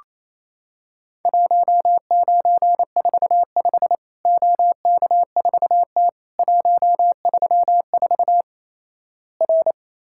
Morse code, heard as a keyed tone, about 28 wpm, 700 Hz. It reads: 1945 OK4T 134 R
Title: Worked All Europe DX Contest - QTC Exchange - 28wpm